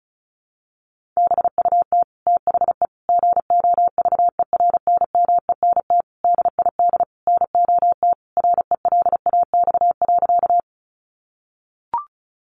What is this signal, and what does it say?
Morse code, heard as a keyed tone, about 34 wpm, 700 Hz. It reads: BUT THE GOVERNMENT DID NOT RELAX.
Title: Sentences from Top 1400 Words 35wpm